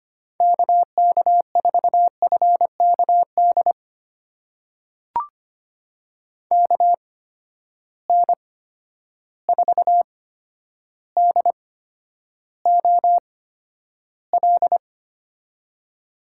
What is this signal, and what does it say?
Morse code, heard as a keyed tone, about 25 wpm, spaced as for 11 wpm, 700 Hz. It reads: KK4FKD KN4DOL
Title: USA Call Signs - 6 Characters - 8x Character Spacing - 25wpm